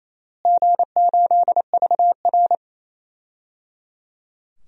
Morse code, heard as a keyed tone, about 28 wpm, 700 Hz. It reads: G8VR
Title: DXCC-100 Call Signs - 4 Characters - Repeated 3x - 28wpm